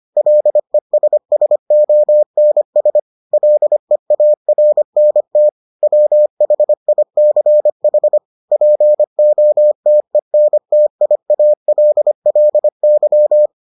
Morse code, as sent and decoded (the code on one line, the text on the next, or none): .-.. ... ... --- -. ... / .-.. .- .-. -. - / .-- .... .. -.-. .... / .--. --- - . -. - .. .- .-.. .-.. -.--
LESSONS LEARNT WHICH POTENTIALLY